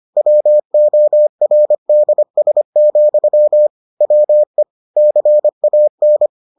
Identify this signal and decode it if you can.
Morse code, heard as a keyed tone, about 25 wpm, 600 Hz. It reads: WORDS, WE CAN